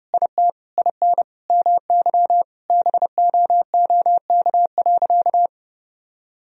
Morse code, heard as a keyed tone, about 30 wpm, 700 Hz. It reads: IT IN MY BOOK.
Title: Sentences from Top 1000 Words - Review Part 1 - 30wpm